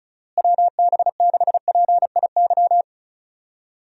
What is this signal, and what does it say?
Morse code, heard as a keyed tone, about 36 wpm, 700 Hz. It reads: WB6PIY
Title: USA Call Signs - 6 Characters - Repeated 2x - 35wpm